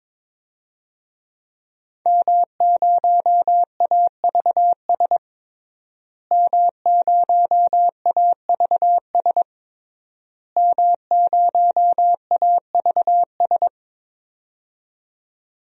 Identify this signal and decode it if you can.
Morse code, heard as a keyed tone, about 22 wpm, 700 Hz. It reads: M0AVS M0AVS M0AVS
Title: DXCC-100 Call Signs - 5 Characters - Repeated 3x - 22wpm